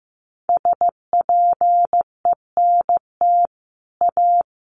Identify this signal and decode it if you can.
Morse code, heard as a keyed tone, about 15 wpm, 700 Hz. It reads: SPENT A
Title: Sets of 5 Words 15wpm